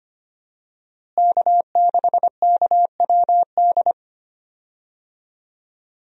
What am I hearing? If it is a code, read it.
Morse code, heard as a keyed tone, about 25 wpm, 700 Hz. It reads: K6KWD
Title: USA Call Signs - 5 Characters - Repeated 3x - 25wpm